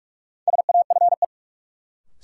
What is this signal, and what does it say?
Morse code, heard as a keyed tone, about 46 wpm, 700 Hz. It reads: SAFE
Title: Sentences from Top 900 Words - New Word Drill - 45wpm